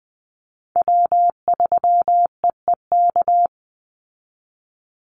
Morse code, heard as a keyed tone, about 20 wpm, 700 Hz. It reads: W3EEK
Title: Call Signs Repeated 3x - 20wpm